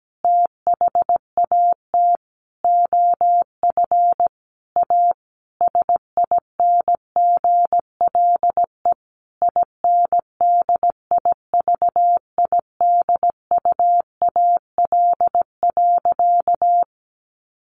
Morse code, heard as a keyed tone, about 17 wpm, 700 Hz. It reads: THAT OF A SINGLE INDIVIDUAL.
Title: Sentences from Top 1500 Words 17wpm